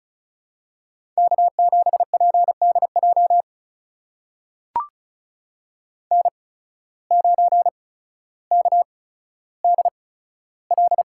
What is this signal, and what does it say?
Morse code, heard as a keyed tone, about 36 wpm, spaced as for 15 wpm, 700 Hz. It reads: K7PDJ N9KDL
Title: USA Call Signs - 5 Characters - 8x Character Spacing - 35wpm